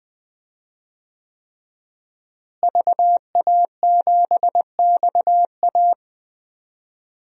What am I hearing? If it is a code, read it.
Morse code, heard as a keyed tone, about 20 wpm, 700 Hz. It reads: VA7XA
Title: DXCC-100 Call Signs - 5 Characters - Repeated 3x - 20wpm